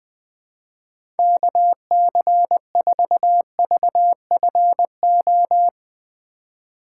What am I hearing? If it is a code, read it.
Morse code, heard as a keyed tone, about 21 wpm, 700 Hz. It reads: KC4VFO
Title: USA Call Signs - 2x Character Spacing - 20wpm